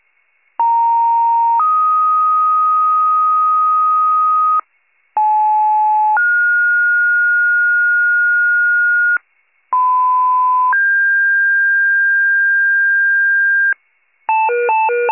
Two-tone page, heard as a one-second tone, then a three-second tone, about 930 then 1250 Hz, about 860 then 1430 Hz, about 1010 then 1600 Hz, with noise. It is heard from 0.6 to 13.7 s.